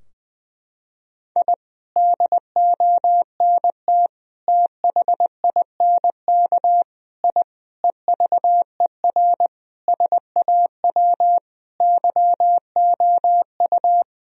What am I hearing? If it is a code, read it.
Morse code, heard as a keyed tone, about 20 wpm, 700 Hz. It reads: I DONT THINK I EVER SAW YOU
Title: Sentences from Top 1000 Words - Review Part 4 - 20wpm